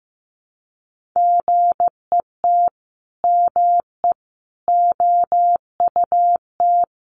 Morse code, heard as a keyed tone, about 15 wpm, 700 Hz. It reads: GET ME OUT
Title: A Sound Of Thunder 15wpm